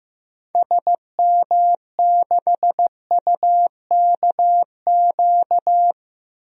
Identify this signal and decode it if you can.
Morse code, heard as a keyed tone, about 15 wpm, 700 Hz. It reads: SM6UKQ
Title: DXCC-100 Call Signs - 6 Characters - 15wpm